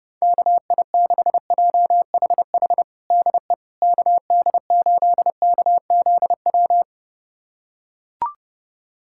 Morse code, heard as a keyed tone, about 30 wpm, 700 Hz. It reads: KI6JHH DE KD8KZW
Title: USA Callsign DE Callsign 30wpm